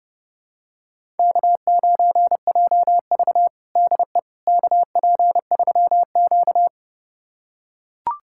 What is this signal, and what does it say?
Morse code, heard as a keyed tone, about 30 wpm, 700 Hz. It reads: K9JV DE KP3Q
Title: USA Callsign DE Callsign - 4 Characters - 30wpm